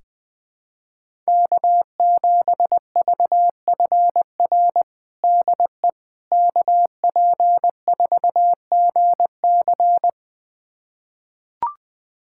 Morse code, heard as a keyed tone, about 21 wpm, 700 Hz. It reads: K7VFR DE KP4GC
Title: USA Callsign DE Callsign - 5 Characters - 20wpm